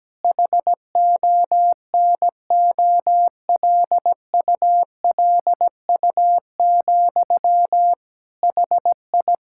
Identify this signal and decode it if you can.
Morse code, heard as a keyed tone, about 17 wpm, 700 Hz. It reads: HONOLULU, HI
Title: Practice QSO - Exchange 1 - Beginning - 17wpm